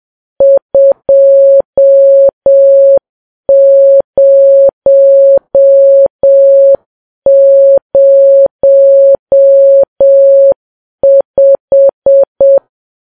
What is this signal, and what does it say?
Morse code, heard as a keyed tone, about 7 wpm, 550 Hz. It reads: 2005
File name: Files Web tls / Morse on9fv / morse7kps